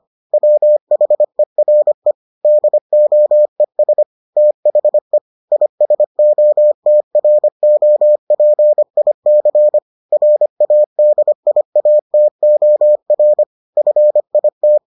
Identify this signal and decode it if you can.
Morse code, heard as a keyed tone, about 25 wpm, 600 Hz. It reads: WHERE DOES THE ISOTROPIC RADIATOR FIT